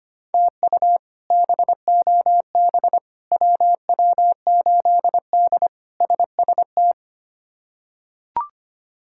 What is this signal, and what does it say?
Morse code, heard as a keyed tone, about 25 wpm, 700 Hz. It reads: TU BOB WW8D SST
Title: K1USN SST QSO Practice - 25wpm